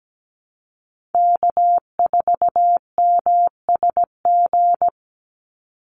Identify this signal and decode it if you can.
Morse code, heard as a keyed tone, about 17 wpm, 700 Hz. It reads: K4MSG
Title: Call Signs Repeated 2x - 17wpm